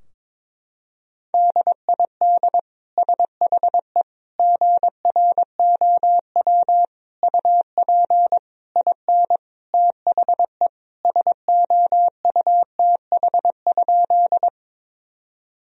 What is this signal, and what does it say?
Morse code, heard as a keyed tone, about 22 wpm, 700 Hz. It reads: DID SHE GROW UP IN THE SOUTH?
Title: Sentences from Top 1000 Words - Review Part 2 - 22wpm